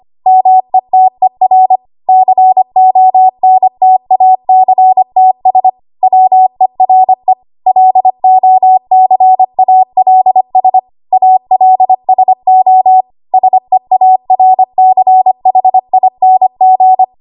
Morse code, heard as a keyed tone, about 25 wpm, 750 Hz. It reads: METER CONTACTS WERE LOCALS ALSO SEARCHING
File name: Archive / 25 WPM / 25WPM